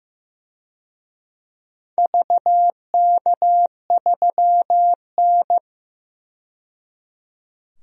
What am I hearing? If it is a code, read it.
Morse code, heard as a keyed tone, about 15 wpm, 700 Hz. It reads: VK3N
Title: DXCC-100 Call Signs - 4 Characters - 15wpm